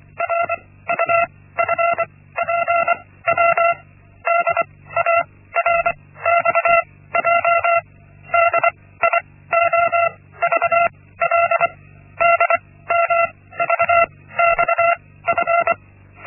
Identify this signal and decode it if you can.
Morse code, heard as a keyed tone, about 24 wpm, spaced as for 16 wpm, 680 Hz. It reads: RUFPW DARXJ DIOVL DMVXF
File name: Linksmakalnis-1